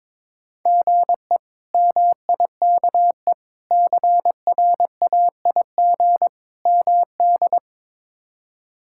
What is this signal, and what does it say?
Morse code, heard as a keyed tone, about 22 wpm, 700 Hz. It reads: GE MIKE CRAIG MD
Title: K1USN SST QSO Practice - 22wpm